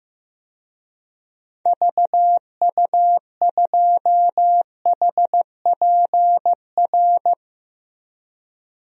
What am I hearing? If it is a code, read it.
Morse code, heard as a keyed tone, about 15 wpm, 700 Hz. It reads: VU2HPR
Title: DXCC-100 Call Signs - 6 Characters - Repeated 2x - 15wpm